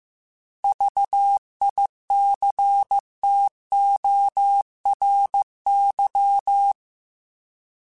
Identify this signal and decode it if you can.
Morse code, heard as a keyed tone, about 15 wpm, 800 Hz. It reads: VICTORY
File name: Alphabetical / Alphabetical mp3s / Vs @13